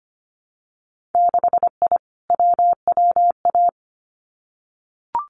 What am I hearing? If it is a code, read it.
Morse code, heard as a keyed tone, about 25 wpm, 700 Hz. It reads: 6I WWA
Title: Winter Field Day Exchanges Encore 25wpm